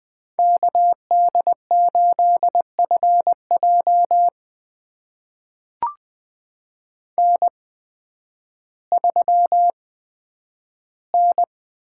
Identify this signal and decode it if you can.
Morse code, heard as a keyed tone, about 20 wpm, spaced as for 11 wpm, 700 Hz. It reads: KD8FJ N3N
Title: USA Call Signs - 5 Characters - 8x Character Spacing - 20wpm